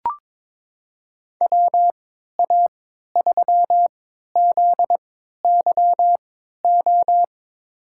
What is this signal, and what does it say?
Morse code, heard as a keyed tone, about 22 wpm, spaced as for 13 wpm, 700 Hz. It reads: WA3ZYO
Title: USA Call Signs - 6 Characters - 3x Character Spacing - 22wpm